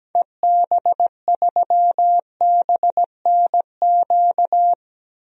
Morse code, heard as a keyed tone, about 17 wpm, 700 Hz. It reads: EB3BNQ
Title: DXCC-100 Call Signs - 6 Characters - Repeated 2x - 17wpm